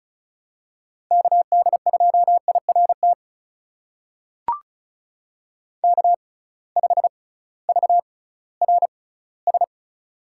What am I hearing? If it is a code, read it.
Morse code, heard as a keyed tone, about 34 wpm, spaced as for 19 wpm, 700 Hz. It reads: KD2IRT K5VRS
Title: USA Call Signs - 6x Character Spacing - 35wpm